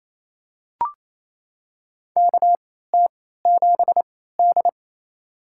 Morse code, heard as a keyed tone, about 27 wpm, spaced as for 16 wpm, 700 Hz. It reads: KT7D
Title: USA Call Signs - 4 Characters - 3x Character Spacing - 28wpm